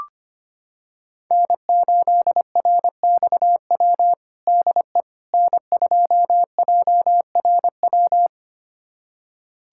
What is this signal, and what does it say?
Morse code, heard as a keyed tone, about 25 wpm, 700 Hz. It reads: N8RXW DE N2JRW
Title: USA Callsign DE Callsign - 5 Characters - 25wpm